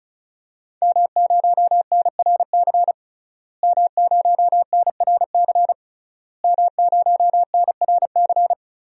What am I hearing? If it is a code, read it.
Morse code, heard as a keyed tone, about 36 wpm, 700 Hz. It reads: M0NRC M0NRC M0NRC
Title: DXCC-100 Call Signs - 5 Characters - Repeated 3x - 35wpm